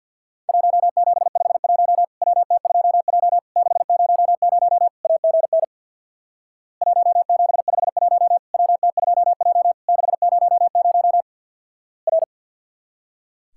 Morse code, heard as a keyed tone, about 50 wpm, 700 Hz. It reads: 1751 WT2J 600 AGN 1751 WT2J 600 R
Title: Worked All Europe DX Contest - QTC Exchange with Repeats - 50wpm